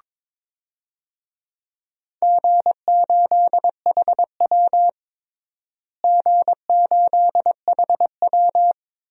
Morse code, heard as a keyed tone, about 21 wpm, 700 Hz. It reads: G8HW G8HW
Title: DXCC-100 Call Signs - 4 Characters - Repeated 2x - 22wpm